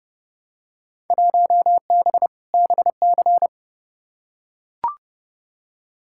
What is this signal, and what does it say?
Morse code, heard as a keyed tone, about 29 wpm, 700 Hz. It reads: 1B BC
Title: Field Day Exchanges - 30wpm